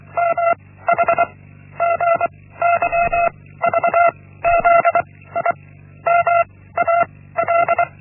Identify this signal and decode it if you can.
Morse code, heard as a keyed tone, about 24 wpm, 680 Hz. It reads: M H G Y V Z I M A L
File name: Linksmakalnis-1